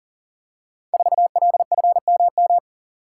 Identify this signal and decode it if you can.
Morse code, heard as a keyed tone, about 38 wpm, 700 Hz. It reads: 4LFMM
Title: Code Groups - 5 Letter:Numbers - 40wpm